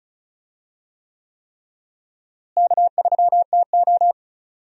Morse code, heard as a keyed tone, about 36 wpm, 700 Hz. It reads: K3TO
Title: USA Call Signs - 4 Characters - Repeated 3x - 35wpm